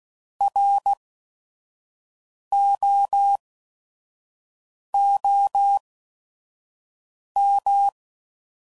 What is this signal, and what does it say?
Morse code, heard as a keyed tone, about 16 wpm, spaced as for 5 wpm, 800 Hz. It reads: ROOM
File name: Prefixes / By First Letter / Rs